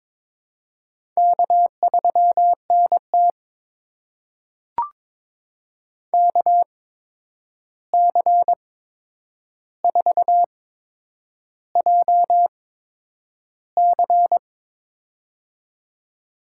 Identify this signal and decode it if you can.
Morse code, heard as a keyed tone, about 23 wpm, spaced as for 9 wpm, 700 Hz. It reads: K3NT KC4JC
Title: USA Call Signs - 8x Character Spacing - 22wpm